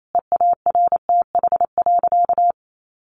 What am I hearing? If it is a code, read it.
Morse code, heard as a keyed tone, about 28 wpm, 700 Hz. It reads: EARTH.
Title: A Sound Of Thunder 28wpm